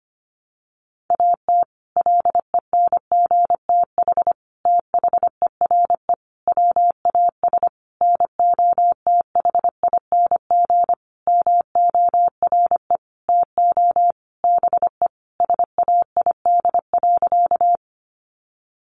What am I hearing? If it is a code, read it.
Morse code, heard as a keyed tone, about 25 wpm, 700 Hz. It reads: AT LENGTH THERE WAS NOTHING MORE TO BE SAID.
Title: Sentences from Top 1000 Words 25wpm